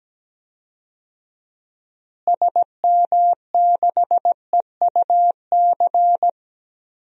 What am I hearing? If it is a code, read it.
Morse code, heard as a keyed tone, about 17 wpm, 700 Hz. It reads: SM6EUC
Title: DXCC-100 Call Signs - 6 Characters - Repeated 2x - 17wpm